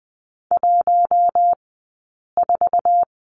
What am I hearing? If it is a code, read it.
Morse code, heard as a keyed tone, about 20 wpm, 700 Hz. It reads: 1 4
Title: 059 Lesson 11d - Letter h - Two Characters 20wpm